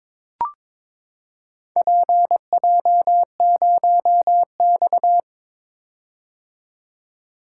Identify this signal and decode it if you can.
Morse code, heard as a keyed tone, about 22 wpm, 700 Hz. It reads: PJ0X
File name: practice / 22wpm / Four Letter-Number Combinations - 22wpm